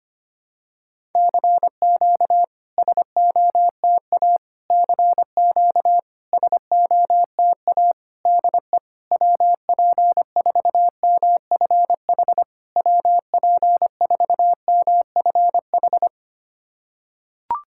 Morse code, heard as a keyed tone, about 25 wpm, 700 Hz. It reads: CQ SOTA CQ SOTA DE WP4MFH WP4MFH
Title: SOTA QSO Practice 25wpm